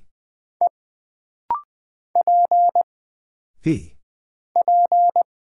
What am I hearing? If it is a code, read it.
Morse code, heard as a keyed tone, about 20 wpm, 700 Hz. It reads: E P P
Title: Single Character (Letter:Number:Punctuation:Prosign) - Rapid-Fire - 20wpm